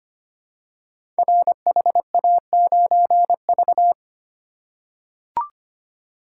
Morse code, heard as a keyed tone, about 25 wpm, 700 Hz. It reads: RHA9V
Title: Code Groups - 5 Letter:Numbers - 25wpm